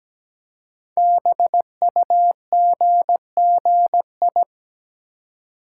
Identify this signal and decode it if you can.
Morse code, heard as a keyed tone, about 17 wpm, 700 Hz. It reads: BUGGI
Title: Code Groups - 5 Letter:Numbers - 17wpm